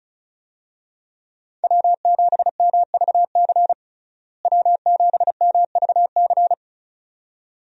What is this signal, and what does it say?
Morse code, heard as a keyed tone, about 34 wpm, 700 Hz. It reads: W7MVC W7MVC
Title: USA Call Signs - 5 Characters - Repeated 2x - 35wpm